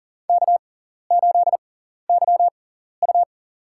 Morse code, heard as a keyed tone, about 39 wpm, spaced as for 14 wpm, 700 Hz. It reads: K8YU
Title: USA Call Signs - 4 Characters - 6x Character Spacing - 40wpm